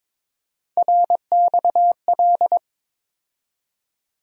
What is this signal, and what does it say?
Morse code, heard as a keyed tone, about 22 wpm, 700 Hz. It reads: RXL
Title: USA Call Sign Postfixes - 3 Letters 22wpm